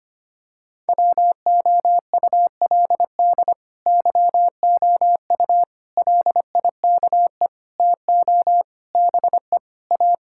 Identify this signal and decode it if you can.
Morse code, heard as a keyed tone, about 25 wpm, 700 Hz. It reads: WOULD YOU LIKE TO BE A
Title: Sentences from Top 1000 Words - Review Part 6 - 25wpm